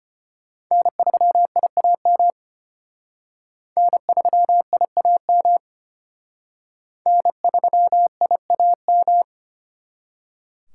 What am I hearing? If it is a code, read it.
Morse code, heard as a keyed tone, about 32 wpm, 700 Hz. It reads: N3IAM N3IAM N3IAM